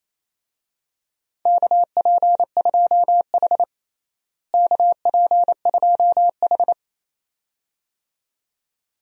Morse code, heard as a keyed tone, about 28 wpm, 700 Hz. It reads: KP2H KP2H
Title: DXCC-100 Call Signs - 4 Characters - Repeated 2x - 28wpm